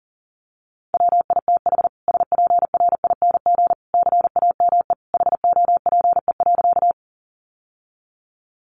Morse code, heard as a keyed tone, about 40 wpm, 700 Hz. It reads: WITH SPRING CAME HOPE.
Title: Sentences from Top 1200 Words 40wpm